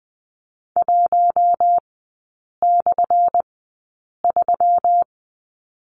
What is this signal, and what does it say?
Morse code, heard as a keyed tone, about 20 wpm, spaced as for 8 wpm, 700 Hz. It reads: 1/3
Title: 178 Lesson 29e - Slash - Three Characters 20wpm